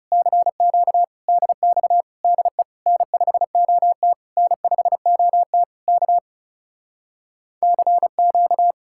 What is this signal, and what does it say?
Morse code, heard as a keyed tone, about 38 wpm, 700 Hz. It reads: CQ DX DE N5OT N5OT K CQ
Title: Speed-Racing Common QSO Phrases - 30wpm